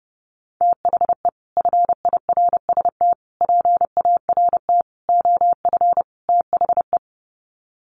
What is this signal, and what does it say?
Morse code, heard as a keyed tone, about 30 wpm, 700 Hz. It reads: THE FIRST PART OF THE
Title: Sets of 5 Words Encore 30wpm